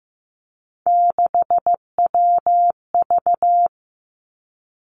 Morse code, heard as a keyed tone, about 15 wpm, 700 Hz. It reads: 6WV
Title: Three Challenging Letter-Number Combinations 15wpm